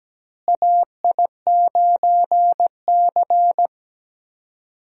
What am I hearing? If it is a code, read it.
Morse code, heard as a keyed tone, about 17 wpm, 700 Hz. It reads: AI9C
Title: USA Call Signs - 4 Characters - Repeated 3x - 17wpm